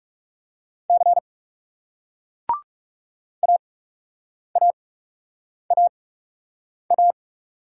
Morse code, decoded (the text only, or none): C AAAA